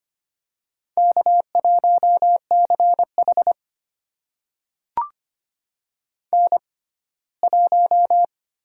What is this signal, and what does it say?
Morse code, heard as a keyed tone, about 25 wpm, spaced as for 17 wpm, 700 Hz. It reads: K1CH N1
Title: USA Call Signs - 6x Character Spacing - 25wpm